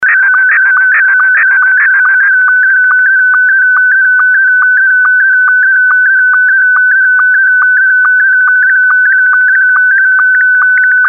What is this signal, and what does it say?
SSTV (slow-scan television), Scottie 1.